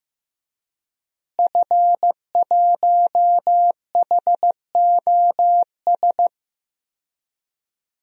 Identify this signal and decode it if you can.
Morse code, heard as a keyed tone, about 15 wpm, 700 Hz. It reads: F1HOS
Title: DXCC-100 Call Signs - 5 Characters - Repeated 2x - 15wpm